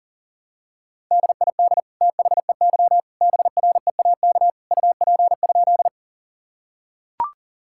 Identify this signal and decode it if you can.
Morse code, heard as a keyed tone, about 40 wpm, 700 Hz. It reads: DID THEY BREAK UP?
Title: Sentences from Top 1000 Words - Review Part 6 - 40wpm